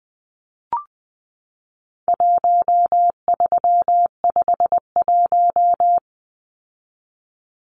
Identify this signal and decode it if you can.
Morse code, heard as a keyed tone, about 20 wpm, 700 Hz. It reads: 1351
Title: Numbers - 4-Digit 20wpm